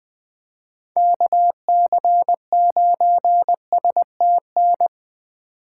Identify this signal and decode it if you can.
Morse code, heard as a keyed tone, about 21 wpm, 700 Hz. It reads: KC9STN
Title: USA Call Signs - 6 Characters - 4x Character Spacing - 20wpm